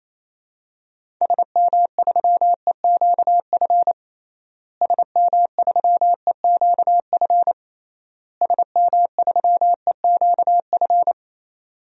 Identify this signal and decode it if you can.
Morse code, heard as a keyed tone, about 28 wpm, 700 Hz. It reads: SM3EQF SM3EQF SM3EQF
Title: DXCC-100 Call Signs - 6 Characters - Repeated 3x - 28wpm